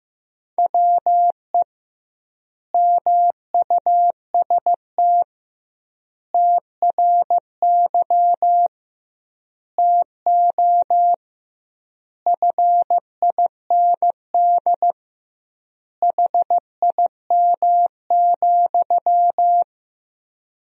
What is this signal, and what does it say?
Morse code, heard as a keyed tone, about 15 wpm, 700 Hz. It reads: WE MUST TRY TO FIND HIM,
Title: 2x Word-Spacing - Sentences from Top 400 Words - 15wpm